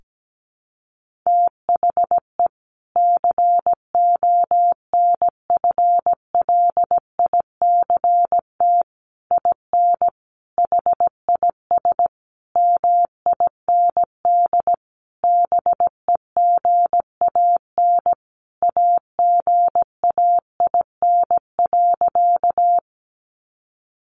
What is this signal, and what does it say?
Morse code, heard as a keyed tone, about 17 wpm, 700 Hz. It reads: THE CONFLICT IN HIS MIND BEGAN AGAIN.
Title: Sentences from Top 1300 Words 17wpm